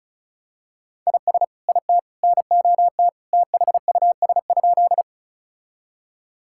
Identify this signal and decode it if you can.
Morse code, heard as a keyed tone, about 36 wpm, 700 Hz. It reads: IS IT NOT THUS?
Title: Sentences from Top 1000 Words - Review Part 5 - 35wpm